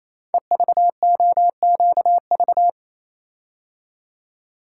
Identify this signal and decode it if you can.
Morse code, heard as a keyed tone, about 27 wpm, 700 Hz. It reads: EVOQV
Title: Code Groups - 5 Letters - 28wpm